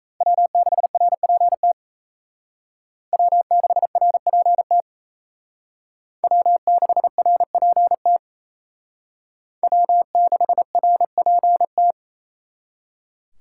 Morse code, heard as a keyed tone, about 35 wpm, 700 Hz. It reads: W6RPT W6RPT W6RPT W6RPT